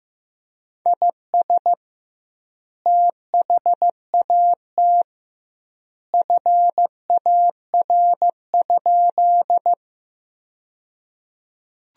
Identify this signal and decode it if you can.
Morse code, heard as a keyed tone, about 15 wpm, 700 Hz. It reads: IS THAT FAR?
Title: 2x Word-Spacing - Sentences from Top 300 Words - 15wpm